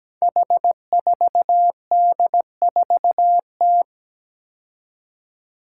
Morse code, heard as a keyed tone, about 17 wpm, 700 Hz. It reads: H4D4T
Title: Five Challenging Letter-Number Combinations 17wpm